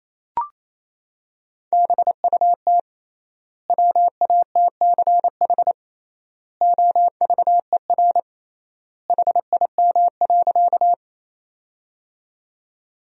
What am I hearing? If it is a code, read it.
Morse code, heard as a keyed tone, about 28 wpm, 700 Hz. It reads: BUT WATCH OVER HIM.
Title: 3x Word-Spacing - Sentences from Top 300 Words - 28wpm